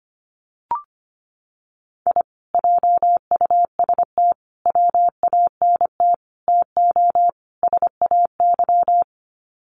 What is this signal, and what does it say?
Morse code, heard as a keyed tone, about 25 wpm, 700 Hz. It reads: I JUST WANT TO SAY
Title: Sets of 5 Words Encore 25wpm